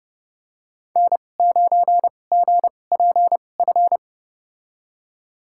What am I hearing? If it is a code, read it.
Morse code, heard as a keyed tone, about 31 wpm, spaced as for 22 wpm, 700 Hz. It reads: N9GPF
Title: USA Call Signs - 5 Characters - 2x Character Spacing - 30wpm